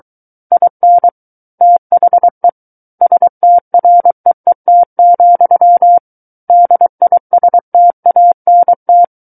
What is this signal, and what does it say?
Morse code, heard as a keyed tone, about 23 wpm, 700 Hz. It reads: IN THE STREET, DISTANT
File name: Book One / 14-Chapter 14 - 23 WPM